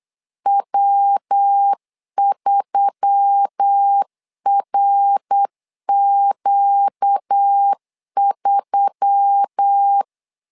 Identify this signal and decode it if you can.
Morse code, heard as a keyed tone, about 8 wpm, 800 Hz. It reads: W3RQ3
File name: LEZIONI / 4 Q1S3 8wpm 3037